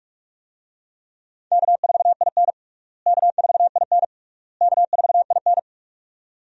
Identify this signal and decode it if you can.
Morse code, heard as a keyed tone, about 46 wpm, 700 Hz. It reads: K4IN K4IN K4IN
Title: USA Call Signs - 4 Characters - Repeated 3x - 45wpm